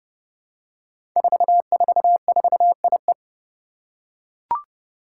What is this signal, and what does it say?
Morse code, heard as a keyed tone, about 28 wpm, 700 Hz. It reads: 444IE
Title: Five Challenging Letter-Number Combinations 30wpm